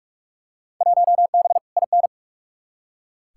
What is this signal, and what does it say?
Morse code, heard as a keyed tone, about 45 wpm, 700 Hz. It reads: 1B IN
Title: Field Day Exchanges Encore - 45wpm